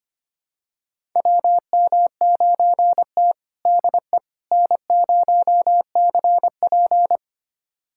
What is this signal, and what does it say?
Morse code, heard as a keyed tone, about 25 wpm, 700 Hz. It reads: WM9T DE N0CP
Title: USA Callsign DE Callsign - 4 Characters - 25wpm